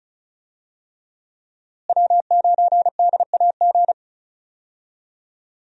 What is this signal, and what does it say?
Morse code, heard as a keyed tone, about 34 wpm, 700 Hz. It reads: W9DAG
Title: USA Call Signs - 5 Characters - Repeated 3x - 35wpm